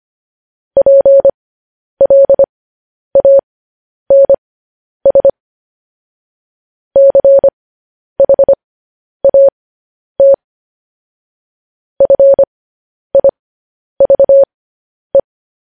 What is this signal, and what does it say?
Morse code, heard as a keyed tone, about 25 wpm, spaced as for 10 wpm, 550 Hz. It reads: PLANS CHAT FIVE
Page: news-headlines-in-morse-code-at-10-wpm